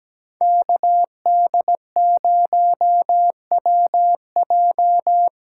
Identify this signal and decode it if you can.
Morse code, heard as a keyed tone, about 19 wpm, 700 Hz. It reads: KD0WJ
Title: Speed Racing - USA Call Signs 17wpm